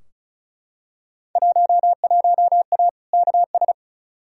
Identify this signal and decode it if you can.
Morse code, heard as a keyed tone, about 34 wpm, 700 Hz. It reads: 11A KS
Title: Field Day Exchanges Encore - 35wpm